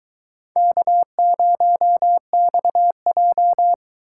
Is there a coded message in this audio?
Yes: Morse code from 0.6 to 3.7 s, heard as a keyed tone, about 22 wpm, 700 Hz. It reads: K0XJ